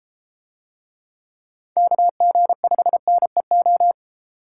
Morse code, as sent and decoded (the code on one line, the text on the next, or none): -.- --. ..... -. ---
KG5NEO